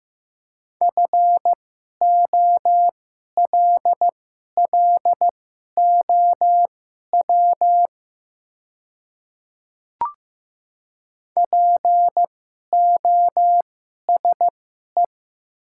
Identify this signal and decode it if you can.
Morse code, heard as a keyed tone, about 15 wpm, spaced as for 11 wpm, 700 Hz. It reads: FOLLOW POSE
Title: 2x Character Spacing - Top 500 Words - 15wpm